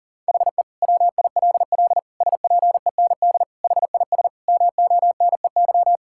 Morse code, heard as a keyed tone, about 37 wpm, 700 Hz. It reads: HE WILL SPEND HIS MONEY
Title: Sentences from Top 1000 Words - Review Part 5 - 40wpm